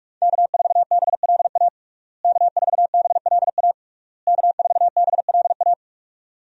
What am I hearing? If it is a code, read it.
Morse code, heard as a keyed tone, about 46 wpm, 700 Hz. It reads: K4BLA K4BLA K4BLA
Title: USA Call Signs - 5 Characters - Repeated 3x - 45wpm